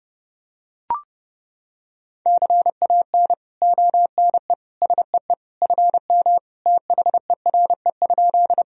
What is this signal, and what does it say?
Morse code, heard as a keyed tone, about 32 wpm, 700 Hz. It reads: CAN ONE SEE FM THERE?
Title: Sentences from Top 100 Words and Abbreviations 30wpm